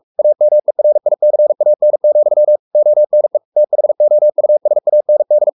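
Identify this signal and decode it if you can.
Morse code, heard as a keyed tone, about 44 wpm, 600 Hz. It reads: AMERICAN, ONE THOUSAND
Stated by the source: Morse code at 43 wpm sending Ken Follett's novel Never in English, from CW med Gnister.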